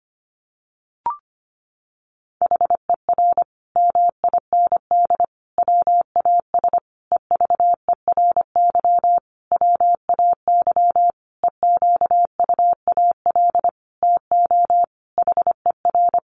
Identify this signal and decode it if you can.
Morse code, heard as a keyed tone, about 25 wpm, 700 Hz. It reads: HER MIND WAS EVERY WAY EQUAL TO HER
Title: Sentences from Top 1400 Words 25wpm